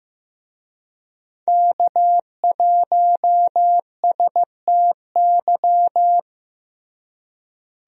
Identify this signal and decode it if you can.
Morse code, heard as a keyed tone, about 15 wpm, 700 Hz. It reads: K1STY
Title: USA Call Signs - 5 Characters - Repeated 3x - 15wpm